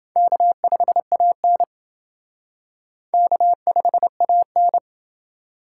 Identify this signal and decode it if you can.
Morse code, heard as a keyed tone, about 28 wpm, 700 Hz. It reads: K5AN K5AN